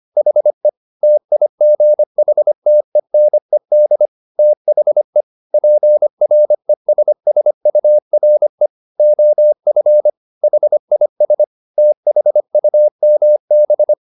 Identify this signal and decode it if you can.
Morse code, heard as a keyed tone, about 25 wpm, 600 Hz. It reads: HE TIGHTENED THE PRESSURE OF HIS THUMB
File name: the old man and the sea part 1 25wpm